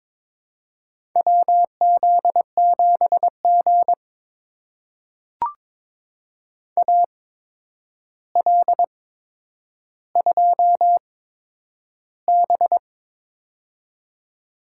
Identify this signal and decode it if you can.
Morse code, heard as a keyed tone, about 22 wpm, spaced as for 9 wpm, 700 Hz. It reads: WZ7G AL2B
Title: USA Call Signs - 4 Characters - 8x Character Spacing - 22wpm